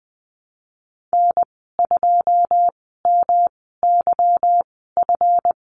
Morse code, heard as a keyed tone, about 20 wpm, spaced as for 14 wpm, 700 Hz. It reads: N2MYF
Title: 161 Lesson 26g - Letter g - Call Signs 20wpm